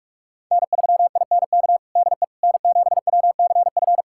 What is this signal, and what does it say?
Morse code, heard as a keyed tone, about 46 wpm, 700 Hz. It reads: N3INK DE N7WCF
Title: USA Callsign DE Callsign - 5 Characters - 45wpm